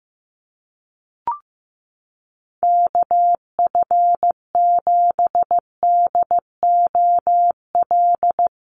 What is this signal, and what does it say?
Morse code, heard as a keyed tone, about 15 wpm, 700 Hz. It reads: KF7DOL
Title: USA Call Signs Repeated 2x 15wpm